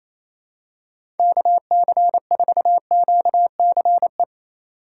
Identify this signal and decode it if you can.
Morse code, heard as a keyed tone, about 28 wpm, 700 Hz. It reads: KC4QCE
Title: USA Call Signs - 6 Characters - 4x Character Spacing - 28wpm